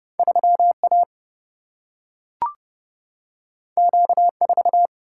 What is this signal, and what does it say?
Morse code, heard as a keyed tone, about 30 wpm, 700 Hz. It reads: 3A Q4
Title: Single-Letter Single-Number 30wpm